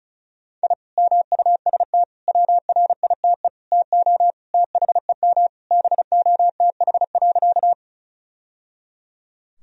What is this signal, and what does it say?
Morse code, heard as a keyed tone, about 36 wpm, 700 Hz. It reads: I MUST WRITE TO THEM BOTH.
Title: Sentences from Top 1000 Words - Review Part 1 - 35wpm